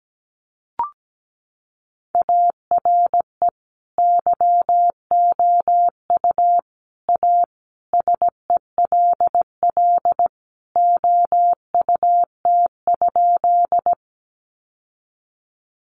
Morse code, heard as a keyed tone, about 17 wpm, 700 Hz. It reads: ARE YOU A SELL OUT?
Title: Sentences from Top 1100 Words 17wpm